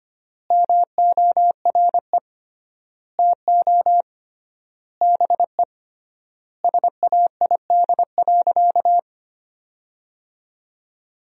Morse code, heard as a keyed tone, about 25 wpm, 700 Hz. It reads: MORE TO BE SAID.
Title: 3x Word-Spacing - Sentences from Top 500 Words - 25wpm